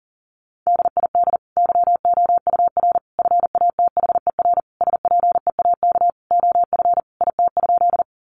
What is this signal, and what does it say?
Morse code, heard as a keyed tone, about 40 wpm, 700 Hz. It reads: DID YOUR FATHER SPEAK OF IT?
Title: Sentences from Top 1100 Words 40wpm